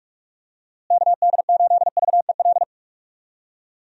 Morse code, heard as a keyed tone, about 45 wpm, 700 Hz. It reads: KD8VEL
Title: USA Call Signs - 6 Characters - Repeated 3x - 45wpm